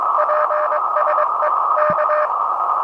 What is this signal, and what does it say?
Morse code, heard as a keyed tone, about 22 wpm, 610 Hz, with noise. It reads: PSE K
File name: attachments / RV3GM CQ